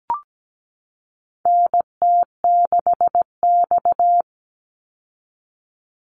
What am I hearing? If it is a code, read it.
Morse code, heard as a keyed tone, about 17 wpm, 700 Hz. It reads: NT6X
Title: Call Signs Repeated 2x - 17wpm